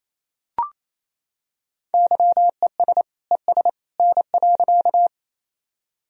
Morse code, heard as a keyed tone, about 28 wpm, 700 Hz. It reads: YES ES N.
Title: Sentences from Top 500 Words and Abbreviations 28wpm